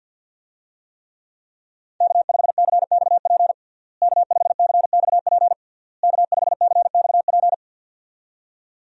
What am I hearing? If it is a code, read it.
Morse code, heard as a keyed tone, about 47 wpm, 690 Hz. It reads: K5CXP K5CXP K5CXP
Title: USA Call Signs - 5 Characters - Repeated 3x - 50wpm